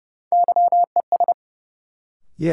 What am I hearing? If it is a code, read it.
Morse code, heard as a keyed tone, about 30 wpm, 700 Hz. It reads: YES
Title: Sentences from Top 500 Words - New Word Drill - 30wpm